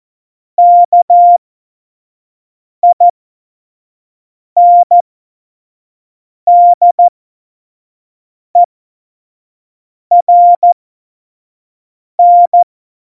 Morse code, heard as a keyed tone, about 14 wpm, spaced as for 5 wpm, 700 Hz. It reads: KINDERN
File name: Morsekurs / Witze / Witz 18. September 2018, 25 F70